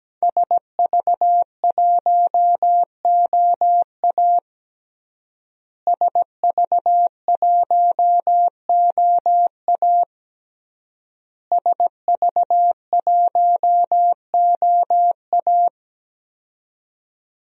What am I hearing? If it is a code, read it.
Morse code, heard as a keyed tone, about 17 wpm, 700 Hz. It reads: SV1OA SV1OA SV1OA
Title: DXCC-100 Call Signs - 5 Characters - Repeated 3x - 17wpm